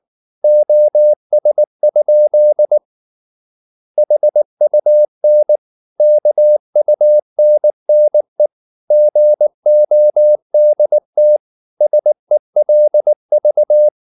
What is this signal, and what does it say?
Morse code, heard as a keyed tone, about 19 wpm, 600 Hz. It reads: OS? HUN KUNNE GODT SELV
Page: Ferie 19 wpm